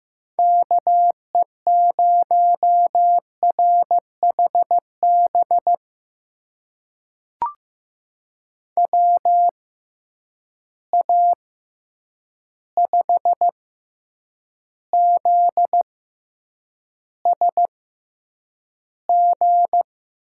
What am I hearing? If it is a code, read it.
Morse code, heard as a keyed tone, about 15 wpm, spaced as for 8 wpm, 700 Hz. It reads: KE0RHB WA5ZSG
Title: USA Call Signs - 6 Characters - 6x Character Spacing - 15wpm